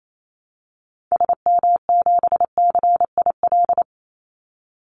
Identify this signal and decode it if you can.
Morse code, heard as a keyed tone, about 28 wpm, 700 Hz. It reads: SM7CIL
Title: Call Signs - 28wpm